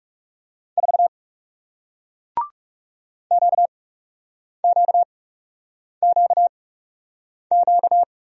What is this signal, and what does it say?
Morse code, heard as a keyed tone, about 39 wpm, spaced as for 8 wpm, 700 Hz. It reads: 4 QQQQ